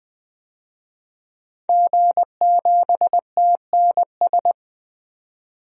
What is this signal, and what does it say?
Morse code, heard as a keyed tone, about 20 wpm, 700 Hz. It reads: G7TNS
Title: DXCC-100 Call Signs - 5 Characters - 20wpm